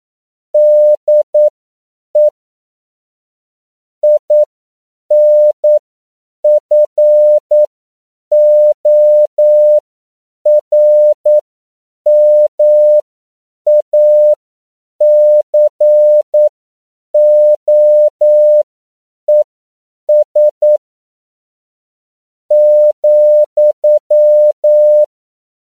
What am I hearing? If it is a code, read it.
Morse code, heard as a keyed tone, about 9 wpm, spaced as for 7 wpm, 600 Hz. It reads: DE INFORMACOES ,
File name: CW / 10PPM